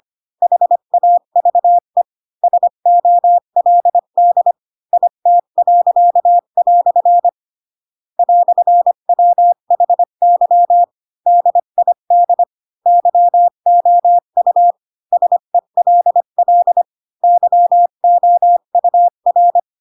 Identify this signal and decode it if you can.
Morse code, heard as a keyed tone, about 25 wpm, 700 Hz. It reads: HAVE SOLD IT." "WHY DID YOU SELL YOUR
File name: Pinocchio / Pinocchio ch8 20 25 ik1hge